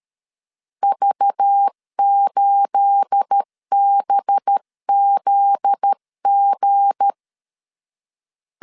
Morse code, heard as a keyed tone, about 13 wpm, 800 Hz. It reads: V8BZG
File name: LEZIONI / 9 28 12wpm 2351